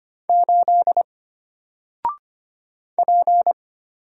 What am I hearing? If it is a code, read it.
Morse code, heard as a keyed tone, about 25 wpm, 700 Hz. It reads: 8 P
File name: Single-Letter-Number-25wpm